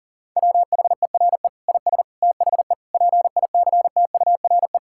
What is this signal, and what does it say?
Morse code, heard as a keyed tone, about 40 wpm, 700 Hz. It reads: WHERE IS THE PICTURE
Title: Sentences from Top 1000 Words - Review Part 6 - 40wpm